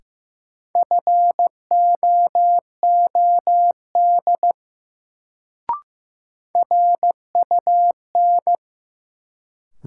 Morse code, heard as a keyed tone, about 15 wpm, 700 Hz. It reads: FOOD RUN
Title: Sentences from Top 300 Words - New Word Drill - 15wpm